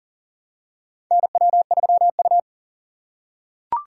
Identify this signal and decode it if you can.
Morse code, heard as a keyed tone, about 40 wpm, 700 Hz. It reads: NW3U
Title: USA Call Signs - 6x Character Spacing - 40wpm